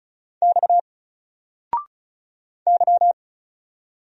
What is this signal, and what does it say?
Morse code, heard as a keyed tone, about 34 wpm, 700 Hz. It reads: X Y
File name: Single-Letter-Number-35wpm